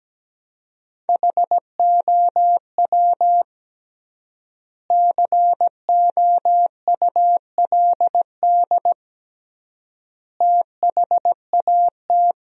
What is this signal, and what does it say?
Morse code, heard as a keyed tone, about 17 wpm, 700 Hz. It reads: HOW COULD THAT
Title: 3x Word-Spacing - Sentences from Top 400 Words - 17wpm